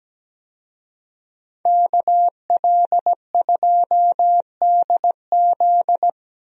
Morse code, heard as a keyed tone, about 17 wpm, 700 Hz. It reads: KL2DZ
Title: USA Call Signs - 5 Characters - Repeated 2x - 17wpm